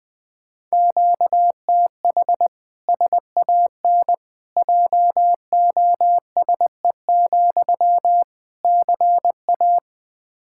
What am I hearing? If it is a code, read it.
Morse code, heard as a keyed tone, about 20 wpm, 700 Hz. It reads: QTH SAN JOSE, CA
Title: Practice QSO - Exchange 1 - Beginning - 20wpm